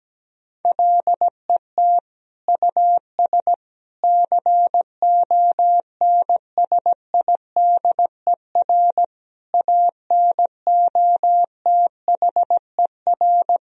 Morse code, heard as a keyed tone, about 17 wpm, 700 Hz. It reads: LET US CONSIDER ANOTHER
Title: Sentences from Top 1000 Words - Review Part 4 - 17wpm